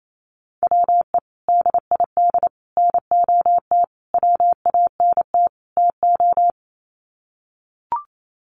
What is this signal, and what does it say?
Morse code, heard as a keyed tone, about 28 wpm, 700 Hz. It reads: WE DID NOT WANT TO
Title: Sets of 5 Words Encore 28wpm